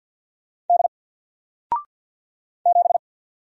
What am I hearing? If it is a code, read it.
Morse code, heard as a keyed tone, about 48 wpm, 700 Hz. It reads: D 7
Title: Single Character (Letter:Number:Punctuation:Prosign) 50wpm